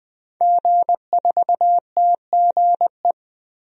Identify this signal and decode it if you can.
Morse code, heard as a keyed tone, about 21 wpm, 700 Hz. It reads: G4TGE
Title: DXCC-100 Call Signs - 5 Characters - Repeated 3x - 20wpm